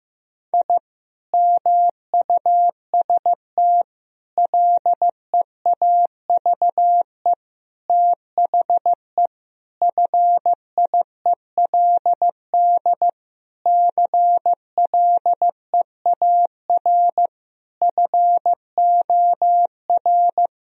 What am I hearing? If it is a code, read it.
Morse code, heard as a keyed tone, about 15 wpm, 700 Hz. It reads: I MUST LEAVE THE FIELD CLEAR FOR
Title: Sentences from Top 1000 Words - Review Part 2 - 15wpm